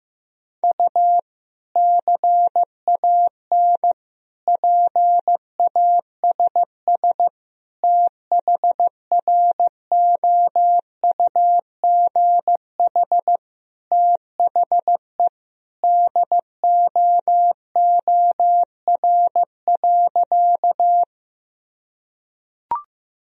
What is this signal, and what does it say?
Morse code, heard as a keyed tone, about 15 wpm, 700 Hz. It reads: U CAN PASS THROUGH THE DOOR.
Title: Sentences from Top 400 Words and Abbreviations 15wpm